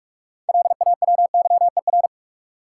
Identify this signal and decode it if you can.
Morse code, heard as a keyed tone, about 46 wpm, 700 Hz. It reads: LAWYER
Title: Six Letter Words - 45wpm